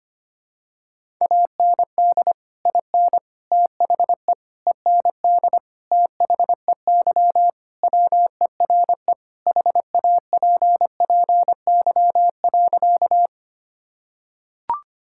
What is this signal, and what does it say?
Morse code, heard as a keyed tone, about 25 wpm, 700 Hz. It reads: AND IN THE END THEY WERE HAPPY.
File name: practice / 25wpm / Sentences with 2-Word Phrases - 25wpm